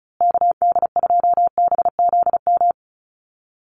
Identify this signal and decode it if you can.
Morse code, heard as a keyed tone, about 34 wpm, 700 Hz. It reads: KD2BZM
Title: USA Call Signs Repeated 2x 35wpm